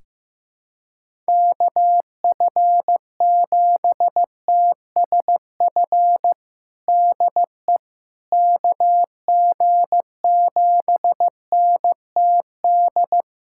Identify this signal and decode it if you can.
Morse code, heard as a keyed tone, about 15 wpm, 700 Hz. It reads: KF7TSF DE KG7NTD
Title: USA Callsign DE Callsign - 6 Characters - 15wpm